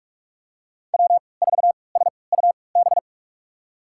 Morse code, heard as a keyed tone, about 46 wpm, spaced as for 26 wpm, 700 Hz. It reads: W4SUB
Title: USA Call Signs - 5 Characters - 3x Character Spacing - 45wpm